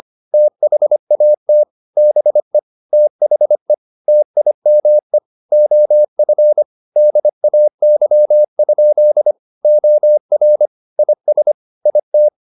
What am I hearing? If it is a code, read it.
Morse code, heard as a keyed tone, about 25 wpm, 600 Hz. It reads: THAT BE THE TIME OF DAY? OR IS IT
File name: the old man and the sea part 1 25wpm